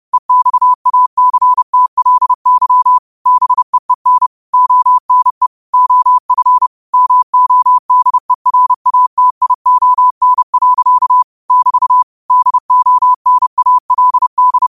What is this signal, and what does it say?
Morse code, heard as a keyed tone, about 30 wpm, 1000 Hz. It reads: EXACTLY BEEN ONE OF MODERATION. <BT> DONALD